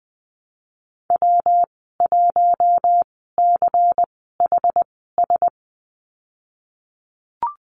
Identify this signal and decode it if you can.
Morse code, heard as a keyed tone, about 20 wpm, spaced as for 15 wpm, 700 Hz. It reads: W1CHS
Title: 122 Lesson 20g - Number 3 - Call Signs 20wpm